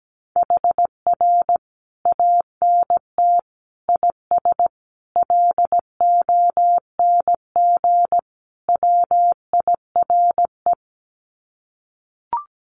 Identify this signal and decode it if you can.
Morse code, heard as a keyed tone, about 17 wpm, 700 Hz. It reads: HR ANT IS LONG WIRE
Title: Common QSO Phrases 17wpm